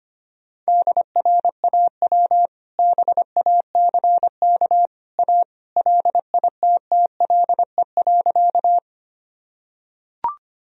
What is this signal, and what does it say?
Morse code, heard as a keyed tone, about 25 wpm, 700 Hz. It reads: DRAW BACK A LITTLE.
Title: Sentences from Top 1000 Words - Review Part 1 - 25wpm